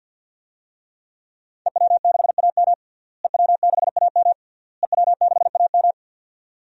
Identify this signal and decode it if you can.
Morse code, heard as a keyed tone, about 50 wpm, 700 Hz. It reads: EW6AM EW6AM EW6AM
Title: DXCC-100 Call Signs - 5 Characters - Repeated 3x - 50wpm